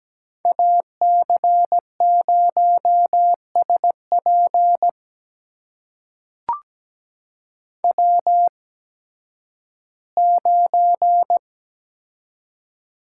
Morse code, heard as a keyed tone, about 17 wpm, spaced as for 11 wpm, 700 Hz. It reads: AC0SP W9
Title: USA Call Signs - 5 Characters - 8x Character Spacing - 17wpm